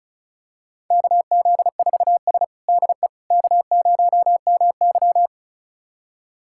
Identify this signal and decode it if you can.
Morse code, heard as a keyed tone, about 36 wpm, 700 Hz. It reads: KZ4S DE K0MY
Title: USA Callsign DE Callsign - 4 Characters - 35wpm